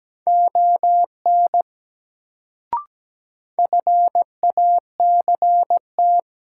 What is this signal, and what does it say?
Morse code, heard as a keyed tone, about 17 wpm, 700 Hz. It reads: ON FACT
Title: Sets of 2 Words - New Word Drill - 17wpm